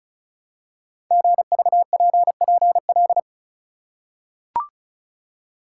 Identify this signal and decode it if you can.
Morse code, heard as a keyed tone, about 36 wpm, 700 Hz. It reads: GVPPL
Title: Code Groups - 5 Letters - 35wpm